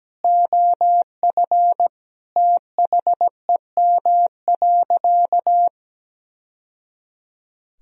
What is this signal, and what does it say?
Morse code, heard as a keyed tone, about 17 wpm, 700 Hz. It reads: OF THEM.
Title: Sentences from Top 1000 Words - Review Part 2 - 17wpm